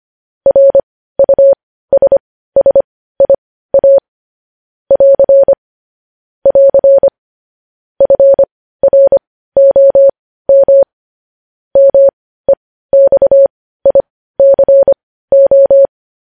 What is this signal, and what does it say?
Morse code, heard as a keyed tone, about 25 wpm, spaced as for 15 wpm, 550 Hz. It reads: RUSSIA <AR> <AR> FROM MEXICO
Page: news headlines in morse code at 15 wpm